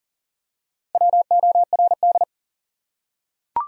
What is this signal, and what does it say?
Morse code, heard as a keyed tone, about 40 wpm, 700 Hz. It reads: WORD
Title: Top 100 Words Repeated 40wpm